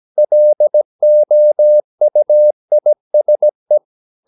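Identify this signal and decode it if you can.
Morse code, heard as a keyed tone, about 17 wpm, 600 Hz. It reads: LOUISE
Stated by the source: Slow (QRS) Morse practice at 17 wpm, from CW med Gnister.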